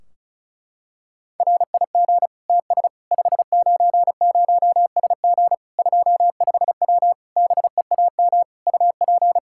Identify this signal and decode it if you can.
Morse code, heard as a keyed tone, about 34 wpm, 700 Hz. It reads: RIG TS 590SG 25W BEAM UP
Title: Practice QSO - Exchange 2 - Info - 35wpm